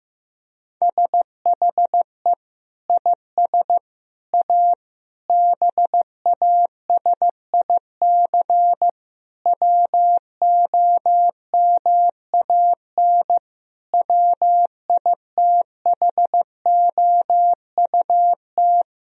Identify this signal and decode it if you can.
Morse code, heard as a keyed tone, about 15 wpm, 700 Hz. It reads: SHE IS A BASIC WOMAN WITHOUT
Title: Sentences from Top 1000 Words - Review Part 3 - 15wpm